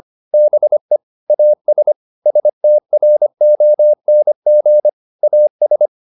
Morse code, heard as a keyed tone, about 25 wpm, 600 Hz. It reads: BE AS STRONG AS
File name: the old man and the sea part 1 25wpm